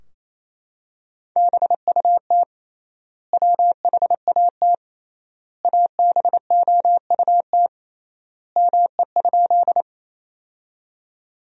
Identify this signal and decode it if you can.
Morse code, heard as a keyed tone, about 28 wpm, 700 Hz. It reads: BUT WHAT ABOUT ME?
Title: 3x Word-Spacing - Sentences from Top 200 Words - 28wpm